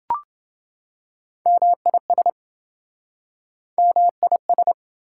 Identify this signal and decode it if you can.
Morse code, heard as a keyed tone, about 29 wpm, 700 Hz. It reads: MIS MIS